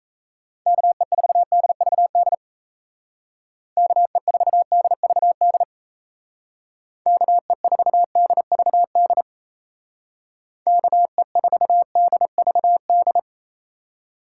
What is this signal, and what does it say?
Morse code, heard as a keyed tone, about 34 wpm, 700 Hz. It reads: KE4DVD KE4DVD KE4DVD KE4DVD
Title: Speed Racing - USA 2x3 Call Signs 28wpm